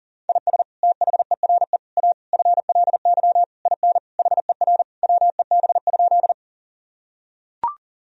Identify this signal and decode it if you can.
Morse code, heard as a keyed tone, about 40 wpm, 700 Hz. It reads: IS THERE A FLY IN HER WEB?
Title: Sentences from Top 1000 Words - Review Part 4 - 40wpm